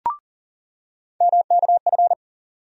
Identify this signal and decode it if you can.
Morse code, heard as a keyed tone, about 40 wpm, 700 Hz. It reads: MKF
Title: USA Call Sign Postfixes - 3 Letters 40wpm